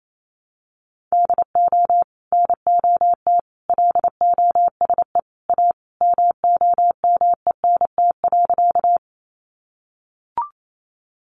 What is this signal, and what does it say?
Morse code, heard as a keyed tone, about 28 wpm, 700 Hz. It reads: DO NOT LOSE A MOMENT.
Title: Sentences from Top 1100 Words 28wpm